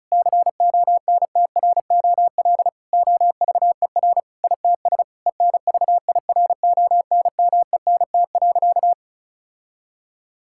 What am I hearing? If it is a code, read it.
Morse code, heard as a keyed tone, about 36 wpm, 700 Hz. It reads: CONTROL OVER ITS ENVIRONMENT.
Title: Sentences from Top 1000 Words - Review Part 5 - 35wpm